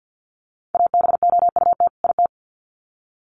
Morse code, heard as a keyed tone, about 50 wpm, 700 Hz. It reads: ABOUT IT
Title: Sets of 2 Words 50wpm